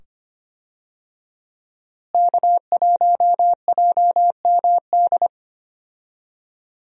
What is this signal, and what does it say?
Morse code, heard as a keyed tone, about 25 wpm, 700 Hz. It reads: K1JMD
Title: USA Call Signs - 5 Characters - Repeated 2x - 25wpm